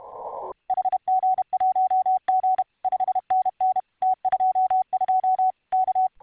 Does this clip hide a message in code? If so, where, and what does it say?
Yes: Morse code from 0.7 to 6.1 s, heard as a keyed tone, about 32 wpm, 750 Hz. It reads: HG1G 5NN T22 K